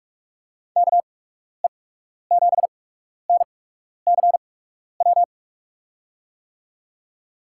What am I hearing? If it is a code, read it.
Morse code, heard as a keyed tone, about 42 wpm, spaced as for 12 wpm, 700 Hz. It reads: KE7NCW
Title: USA Call Signs - 6 Characters - 8x Character Spacing - 45wpm